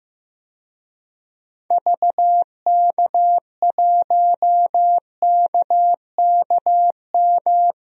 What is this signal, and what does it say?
Morse code, heard as a keyed tone, about 15 wpm, 700 Hz. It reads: VK1KKM
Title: DXCC-100 Call Signs - 6 Characters - Repeated 3x - 15wpm